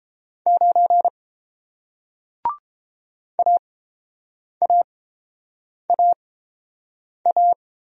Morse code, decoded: 9 AAAA

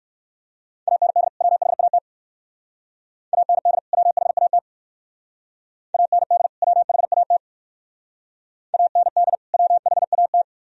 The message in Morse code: AND WHAT AND WHAT AND WHAT AND WHAT

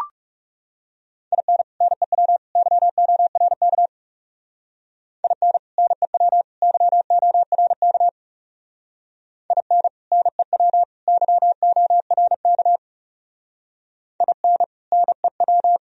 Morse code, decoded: IN NEW YORK IN NEW YORK IN NEW YORK IN NEW

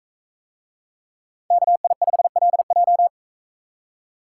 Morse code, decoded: KI5LJ

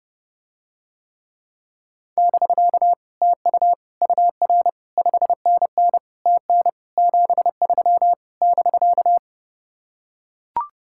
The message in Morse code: <BK> TU UR 5NN TN 73 <BK>

